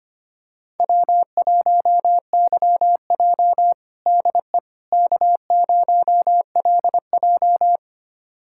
Morse code, heard as a keyed tone, about 25 wpm, 700 Hz. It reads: W1YJ DE K0LJ
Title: USA Callsign DE Callsign - 4 Characters - 25wpm